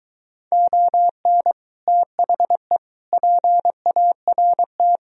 Morse code, heard as a keyed tone, about 24 wpm, 700 Hz. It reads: ON THE PART